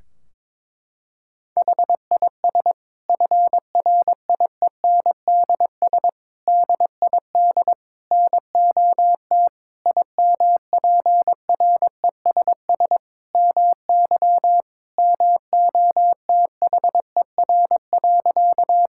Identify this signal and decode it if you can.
Morse code, heard as a keyed tone, about 22 wpm, 700 Hz. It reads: HIS FRIENDS DID NOT IMPRESS MY MOTHER.
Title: Sentences from Top 1600 Words 22wpm